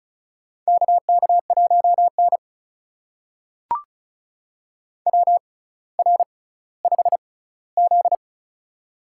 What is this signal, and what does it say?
Morse code, heard as a keyed tone, about 34 wpm, spaced as for 18 wpm, 700 Hz. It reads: KK1N WR5Z
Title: USA Call Signs - 4 Characters - 6x Character Spacing - 35wpm